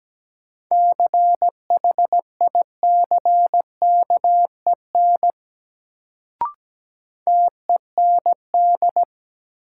Morse code, heard as a keyed tone, about 17 wpm, 700 Hz. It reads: CHICKEN TEND
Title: Sentences from Top 1000 Words - Review Part 5 - New Word Drill - 17wpm